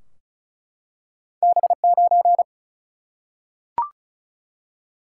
Morse code, heard as a keyed tone, about 36 wpm, 700 Hz. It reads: B9
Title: Single-Letter Single-Number 35wpm